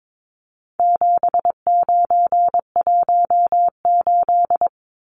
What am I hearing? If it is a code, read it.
Morse code, heard as a keyed tone, about 22 wpm, 700 Hz. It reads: 7918
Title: Numbers - 4-Digit 22wpm